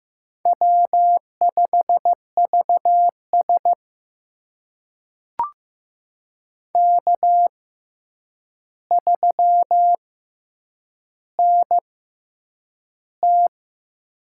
Morse code, heard as a keyed tone, about 15 wpm, spaced as for 8 wpm, 700 Hz. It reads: W5VS K3NT
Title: USA Call Signs - 4 Characters - 6x Character Spacing - 15wpm